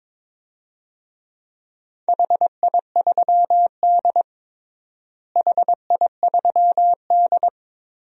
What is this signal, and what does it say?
Morse code, heard as a keyed tone, about 21 wpm, 700 Hz. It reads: HI3D HI3D
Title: DXCC-100 Call Signs - 4 Characters - Repeated 2x - 22wpm